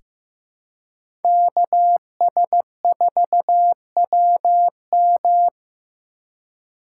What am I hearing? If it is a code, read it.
Morse code, heard as a keyed tone, about 15 wpm, 700 Hz. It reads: KS4WM
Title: USA Call Signs - 2x Character Spacing - 15wpm